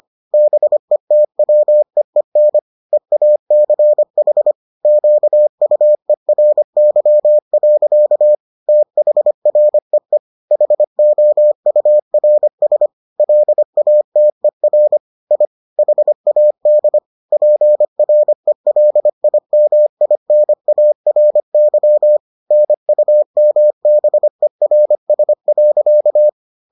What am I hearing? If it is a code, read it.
Morse code, heard as a keyed tone, about 25 wpm, 600 Hz. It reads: BETWEEN EACH QUERY. THREE HOURS LATER I HAD PRELIMINARY NUMBERS.